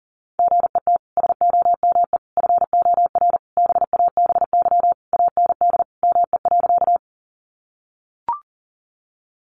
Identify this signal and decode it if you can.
Morse code, heard as a keyed tone, about 40 wpm, 700 Hz. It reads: GET SOME FOR BABY AND ME.
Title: Sentences from Top 1300 Words 40wpm